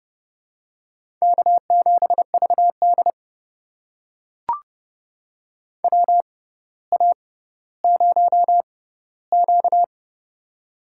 Morse code, heard as a keyed tone, about 30 wpm, spaced as for 15 wpm, 700 Hz. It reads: K7VD WA0Q
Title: USA Call Signs - 4 Characters - 6x Character Spacing - 30wpm